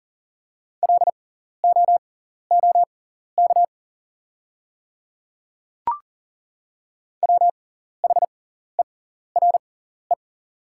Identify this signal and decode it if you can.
Morse code, heard as a keyed tone, about 40 wpm, spaced as for 14 wpm, 690 Hz. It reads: LOOK WHERE
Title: 6x Character Spacing - Top 300 Words - 40wpm